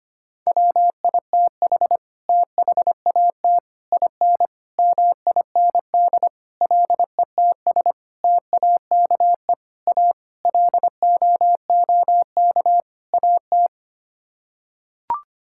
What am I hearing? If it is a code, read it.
Morse code, heard as a keyed tone, about 25 wpm, 700 Hz. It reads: WITH THAT IN MIND LETS TAKE A LOOK AT
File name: Sets of 9 Words - 25wpm